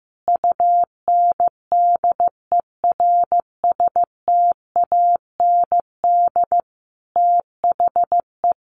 Morse code, heard as a keyed tone, about 15 wpm, 700 Hz. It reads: UNDERSTAND THE
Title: Sentences from Top 1100 Words 15wpm